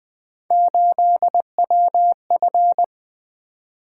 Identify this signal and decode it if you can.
Morse code, heard as a keyed tone, about 20 wpm, 700 Hz. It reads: 8WF
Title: Three Letter-Number Combinations - 20wpm